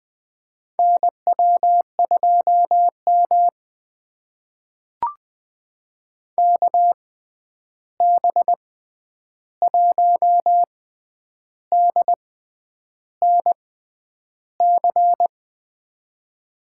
Morse code, heard as a keyed tone, about 21 wpm, spaced as for 9 wpm, 700 Hz. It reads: NW2M KB1DNC
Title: USA Call Signs - 6x Character Spacing - 20wpm